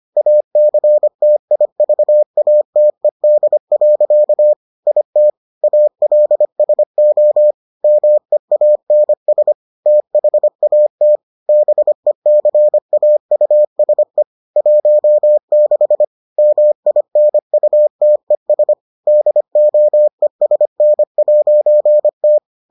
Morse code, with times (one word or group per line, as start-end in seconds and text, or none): 0.2-4.5 ACTIVATED.
4.9-5.3 IT
5.6-7.5 ALSO
7.8-9.5 MEANS
9.9-11.2 THAT
11.5-14.2 BECAUSE
14.6-16.0 16
16.4-18.7 MINUTES
19.1-22.4 DOESN'T